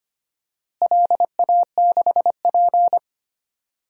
Morse code, heard as a keyed tone, about 25 wpm, 700 Hz. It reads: LA6P
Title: DXCC-100 Call Signs - 4 Characters - Repeated 2x - 25wpm